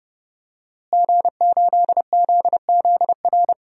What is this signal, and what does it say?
Morse code, heard as a keyed tone, about 32 wpm, 700 Hz. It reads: G8ZZR
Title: DXCC-100 Call Signs - 5 Characters - Repeated 3x - 30wpm